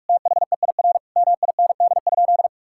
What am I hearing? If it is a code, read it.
Morse code, heard as a keyed tone, about 47 wpm, 700 Hz. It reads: THEIR MIND?
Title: Sentences from Top 1000 Words - Review Part 5 - 45wpm